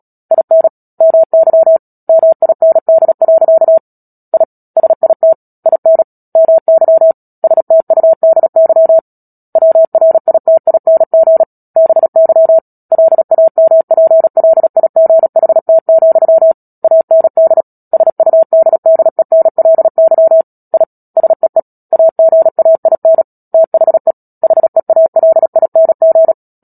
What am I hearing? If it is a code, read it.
Morse code, heard as a keyed tone, about 37 wpm, 650 Hz. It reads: IN MY MIND. I SIT IN MY STUDY WRITING BY LAMPLIGHT, AND SUDDENLY I SEE AGAIN THE HEALING
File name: Book Two / 10-Chapter 10 - Epilogue - 36 WPM